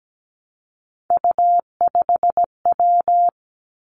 Morse code, heard as a keyed tone, about 17 wpm, 700 Hz. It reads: U5W
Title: Three Challenging Letter-Number Combinations 17wpm